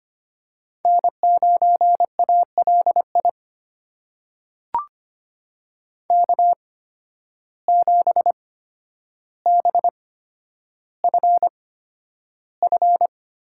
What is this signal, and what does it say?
Morse code, heard as a keyed tone, about 25 wpm, spaced as for 11 wpm, 700 Hz. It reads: N9ALI K7BFF
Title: USA Call Signs - 5 Characters - 8x Character Spacing - 25wpm